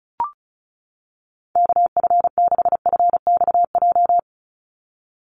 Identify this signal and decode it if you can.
Morse code, heard as a keyed tone, about 34 wpm, 700 Hz. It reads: KF6FXJ
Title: USA Call Signs 35wpm